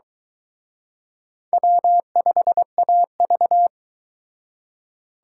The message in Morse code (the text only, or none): W5AV